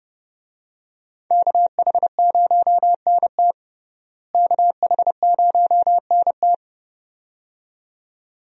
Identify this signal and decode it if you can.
Morse code, heard as a keyed tone, about 31 wpm, 700 Hz. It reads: KH0NT KH0NT
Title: USA Call Signs - 5 Characters - Repeated 2x - 30wpm